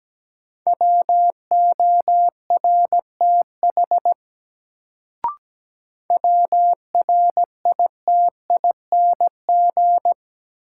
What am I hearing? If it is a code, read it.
Morse code, heard as a keyed tone, about 17 wpm, 700 Hz. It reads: WORTH WRITING
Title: Sentences from Top 800 Words - New Word Drill - 17wpm